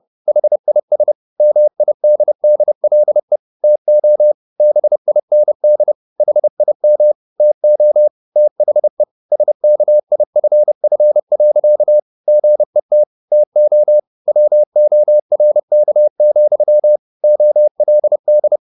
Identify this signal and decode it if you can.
Morse code, heard as a keyed tone, about 30 wpm, 600 Hz. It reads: HIS MIDDLE TO BIND HIM TO THE SKIFF. GET TO WORK, OLD